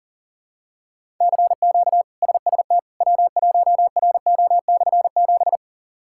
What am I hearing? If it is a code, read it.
Morse code, heard as a keyed tone, about 40 wpm, 700 Hz. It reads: CQ SST W1RO/7
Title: K1USN SST QSO Practice - 40wpm